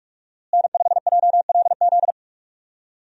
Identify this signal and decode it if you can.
Morse code, heard as a keyed tone, about 42 wpm, 700 Hz. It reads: N5JLZ
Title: USA Call Signs - 5 Characters - Repeated 2x - 45wpm